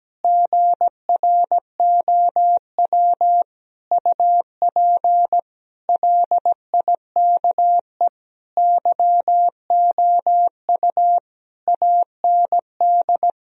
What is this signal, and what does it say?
Morse code, heard as a keyed tone, about 17 wpm, 700 Hz. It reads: GROW UP LIKE YOU AND
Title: Sentences from Top 1000 Words - Review Part 1 - 17wpm